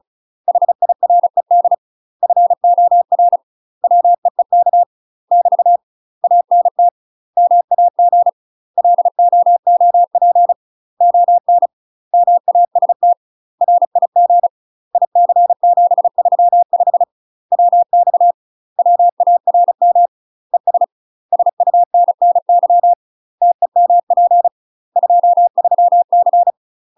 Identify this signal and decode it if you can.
Morse code, heard as a keyed tone, about 34 wpm, 700 Hz. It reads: HIRED FOR WEEK <BT> ANT MAG LOOP ON MAST RIG IC735 WX WARM ES SUNNY TEMP 23C
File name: UK MASTER QSOs TRAINING / 35WPM / UK MASTER QSOs 0037